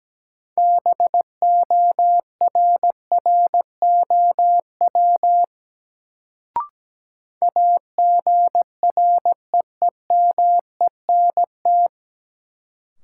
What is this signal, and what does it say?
Morse code, heard as a keyed tone, about 17 wpm, 700 Hz. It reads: BORROW AGREEMENT
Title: Sentences from Top 1500 Words - New Word Drill - 17wpm